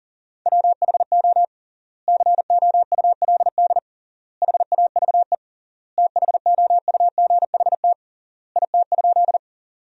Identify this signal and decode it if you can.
Morse code, heard as a keyed tone, about 40 wpm, 700 Hz. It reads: WHO COULD HAVE THOUGHT IT?
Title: 3x Word-Spacing - Sentences from Top 300 Words - 40wpm